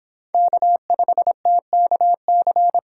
Morse code, heard as a keyed tone, about 31 wpm, 700 Hz. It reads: K5TKC